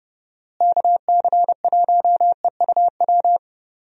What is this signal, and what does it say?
Morse code, heard as a keyed tone, about 30 wpm, 700 Hz. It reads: KC1EUW
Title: USA Call Signs - 6 Characters - Repeated 3x - 30wpm